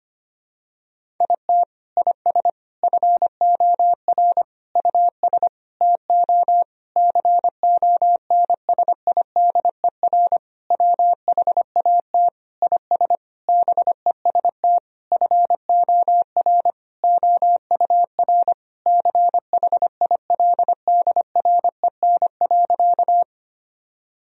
Morse code, heard as a keyed tone, about 25 wpm, 700 Hz. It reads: IT IS FOR US TO CONSIDER WHAT IS BEST FOR OUR CHILDREN.
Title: Sentences from Top 1000 Words - Review Part 4 - 25wpm